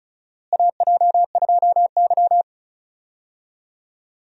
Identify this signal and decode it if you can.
Morse code, heard as a keyed tone, about 36 wpm, 700 Hz. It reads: AJ2Y
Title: USA Call Signs - 4 Characters - Repeated 3x - 35wpm